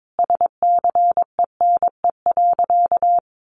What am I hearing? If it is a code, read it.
Morse code, heard as a keyed tone, about 22 wpm, 700 Hz. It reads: SCENE.
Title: Sentences from Top 1100 Words 22wpm